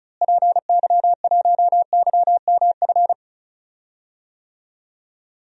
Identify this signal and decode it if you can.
Morse code, heard as a keyed tone, about 34 wpm, 700 Hz. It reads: PY1YMF
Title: DXCC-100 Call Signs - 6 Characters - Repeated 2x - 35wpm